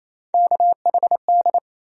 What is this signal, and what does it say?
Morse code, heard as a keyed tone, about 28 wpm, 700 Hz. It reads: KHD
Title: USA Call Sign Postfixes - 3 Letters 28wpm